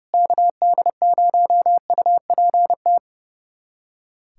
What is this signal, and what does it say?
Morse code, heard as a keyed tone, about 32 wpm, 700 Hz. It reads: KD0UPT
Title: USA Callsign DE Callsign 30wpm